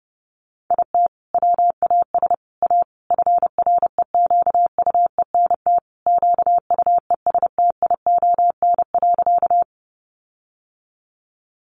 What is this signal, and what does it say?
Morse code, heard as a keyed tone, about 28 wpm, 700 Hz. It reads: IT WAS A FREQUENT QUESTION.
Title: Sentences from Top 1400 Words 30wpm